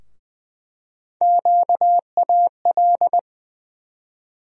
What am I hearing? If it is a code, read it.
Morse code, heard as a keyed tone, about 20 wpm, 700 Hz. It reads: QAL
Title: USA Call Sign Postfixes - 3 Letters 20wpm